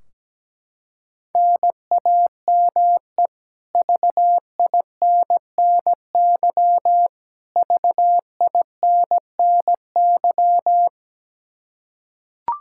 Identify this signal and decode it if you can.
Morse code, heard as a keyed tone, about 17 wpm, 700 Hz. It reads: NAME VINNY VINNY
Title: Practice QSO - Exchange 1 - Beginning - 17wpm